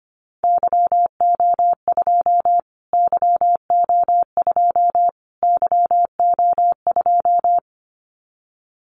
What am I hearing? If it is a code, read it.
Morse code, heard as a keyed tone, about 25 wpm, 700 Hz. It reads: YO2 YO2 YO2
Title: Callsign Prefixes for Top 25 DXCC - Repeated 3x - 25wpm